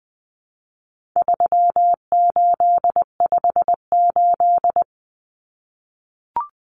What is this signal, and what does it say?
Morse code, heard as a keyed tone, about 20 wpm, 700 Hz. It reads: 3858
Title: Common Cut Numbers - 4-Digit 20wpm